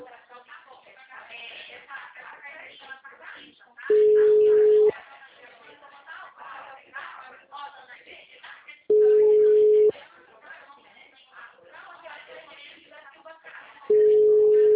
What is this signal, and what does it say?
Ringback tone (Europe, 425 Hz), with speech.
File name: out- -2002- - - .12331